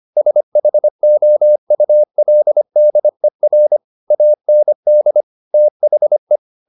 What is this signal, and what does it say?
Morse code, heard as a keyed tone, about 25 wpm, 600 Hz. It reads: SHOULDER AND THE
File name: the old man and the sea part 1 25wpm